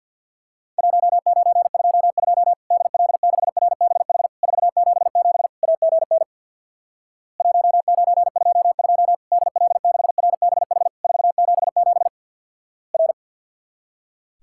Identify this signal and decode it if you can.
Morse code, heard as a keyed tone, about 50 wpm, 700 Hz. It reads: 1922 DL6RBH 477 AGN 1922 DL6RBH 477 R
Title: Worked All Europe DX Contest - QTC Exchange with Repeats - 50wpm